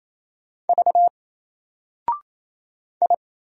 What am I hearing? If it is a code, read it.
Morse code, heard as a keyed tone, about 28 wpm, 700 Hz. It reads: V I
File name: Single-Letter-Number-28wpm